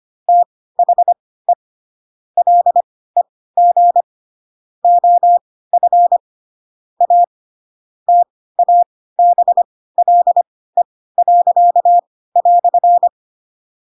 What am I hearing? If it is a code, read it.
Morse code, heard as a keyed tone, about 25 wpm, spaced as for 16 wpm, 700 Hz. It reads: THE LEG OF A TABLE."
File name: Pinocchio / Pinocchio ch1 10 25 ik1hge